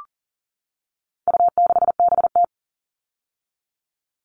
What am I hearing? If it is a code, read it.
Morse code, heard as a keyed tone, about 40 wpm, 700 Hz. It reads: U6BT
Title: All Cut Numbers - 4-Digit 40wpm